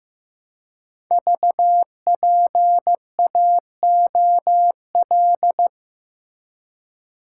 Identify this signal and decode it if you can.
Morse code, heard as a keyed tone, about 15 wpm, 700 Hz. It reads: VPAOL
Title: Code Groups - 5 Letters - 15wpm